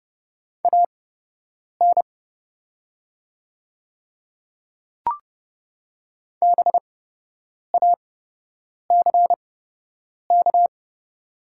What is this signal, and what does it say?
Morse code, heard as a keyed tone, about 30 wpm, spaced as for 8 wpm, 700 Hz. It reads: AN BACK
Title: 8x Character Spacing - Top 200 Words - 30wpm